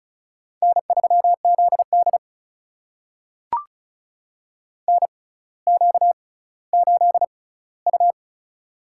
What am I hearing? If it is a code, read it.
Morse code, heard as a keyed tone, about 36 wpm, spaced as for 18 wpm, 700 Hz. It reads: N3ZD NQ8U
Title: USA Call Signs - 4 Characters - 6x Character Spacing - 35wpm